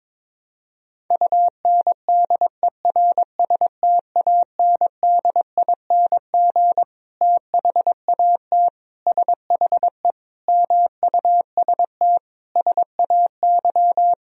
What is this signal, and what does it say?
Morse code, heard as a keyed tone, about 22 wpm, 700 Hz. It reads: UNDERSTANDING THAT SHE MUST SAY
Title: Sentences from Top 1000 Words - Review Part 3 - 22wpm